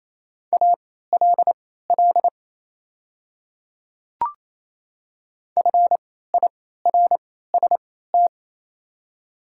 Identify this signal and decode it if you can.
Morse code, heard as a keyed tone, about 27 wpm, spaced as for 16 wpm, 690 Hz. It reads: ALL FIRST
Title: 3x Character Spacing - Top 100 Words - 28wpm